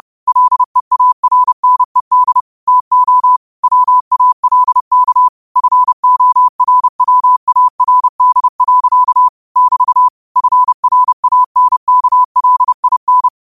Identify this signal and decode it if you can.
Morse code, heard as a keyed tone, about 30 wpm, 1000 Hz. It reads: LEARNED TO WALK FORWARD. <BT> FRANKLIN